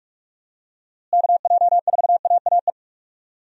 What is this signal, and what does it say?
Morse code, heard as a keyed tone, about 46 wpm, 700 Hz. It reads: KJ4AAE
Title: USA Call Signs - 6 Characters - Repeated 2x - 45wpm